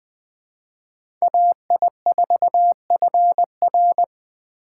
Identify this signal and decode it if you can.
Morse code, heard as a keyed tone, about 20 wpm, 700 Hz. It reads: AI4FR